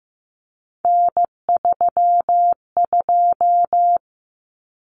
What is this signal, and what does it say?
Morse code, heard as a keyed tone, about 15 wpm, 700 Hz. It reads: N32
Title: Common Cut Numbers - 3-Digit 15wpm